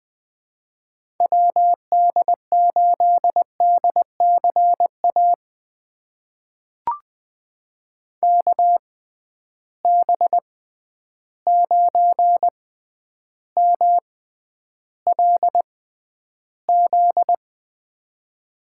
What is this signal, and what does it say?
Morse code, heard as a keyed tone, about 21 wpm, spaced as for 10 wpm, 700 Hz. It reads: WD8DCA KB9MLZ
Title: USA Call Signs - 6 Characters - 6x Character Spacing - 20wpm